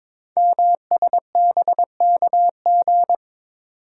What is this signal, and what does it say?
Morse code, heard as a keyed tone, about 22 wpm, 700 Hz. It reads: MSBKG
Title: Code Groups - 5 Letters - 22wpm